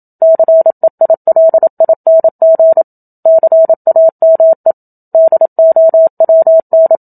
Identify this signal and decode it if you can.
Morse code, heard as a keyed tone, about 27 wpm, 650 Hz. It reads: CEILING CAME DOWN
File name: Book Two / 01-Chapter 1 - 27 WPM